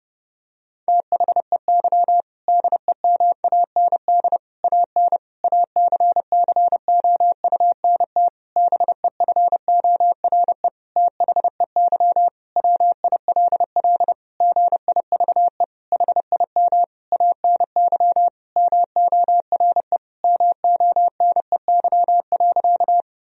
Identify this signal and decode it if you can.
Morse code, heard as a keyed tone, about 30 wpm, 700 Hz. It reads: THEY DEMAND AN ACCOUNT BEFORE THEY WILL GIVE HIM ANY MORE MONEY.
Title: Sentences from Top 1000 Words - Review Part 4 - 30wpm